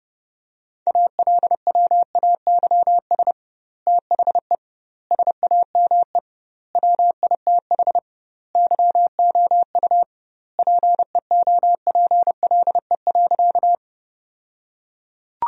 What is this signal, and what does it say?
Morse code, heard as a keyed tone, about 30 wpm, 700 Hz. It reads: ALWAYS THE SAME WITH YOU PEOPLE.
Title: 2x Word-Spacing - Sentences from Top 300 Words - 30wpm